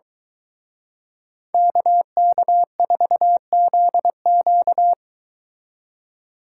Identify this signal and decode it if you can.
Morse code, heard as a keyed tone, about 22 wpm, 700 Hz. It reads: KK4ZQ